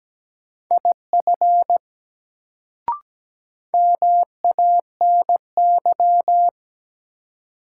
Morse code, heard as a keyed tone, about 17 wpm, 700 Hz. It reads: IF MANY
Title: Sentences from Top 100 Words - New Word Drill - 17wpm